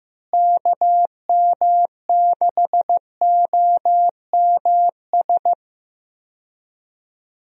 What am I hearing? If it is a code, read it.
Morse code, heard as a keyed tone, about 15 wpm, 700 Hz. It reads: KM6OMS
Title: USA Call Signs - 6 Characters - Repeated 2x - 15wpm